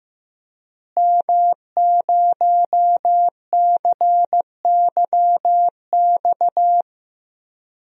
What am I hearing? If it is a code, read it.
Morse code, heard as a keyed tone, about 15 wpm, 700 Hz. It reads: M0CYX
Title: DXCC-100 Call Signs - 5 Characters - Repeated 3x - 15wpm